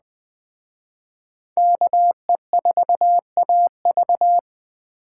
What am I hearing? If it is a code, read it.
Morse code, heard as a keyed tone, about 21 wpm, 700 Hz. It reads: KE4AV
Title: Speed Racing - USA 2x2 Call Signs 17wpm